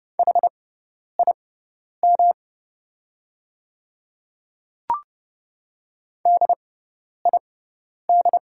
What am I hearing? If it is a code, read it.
Morse code, heard as a keyed tone, about 30 wpm, spaced as for 10 wpm, 700 Hz. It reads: HIM DID
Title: 6x Character Spacing - Top 100 Words - 30wpm